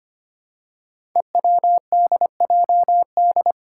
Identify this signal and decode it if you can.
Morse code, heard as a keyed tone, about 25 wpm, 700 Hz. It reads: EWDJD
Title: Five Challenging Letter-Number Combinations 25wpm